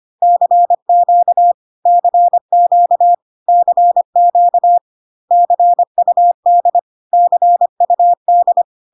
Morse code, heard as a keyed tone, about 25 wpm, 700 Hz. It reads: CQ CQ CQ CUD CUD
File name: cw / CW common abbreviations alphabetical repeated 3 times 25 25wpm ik1hge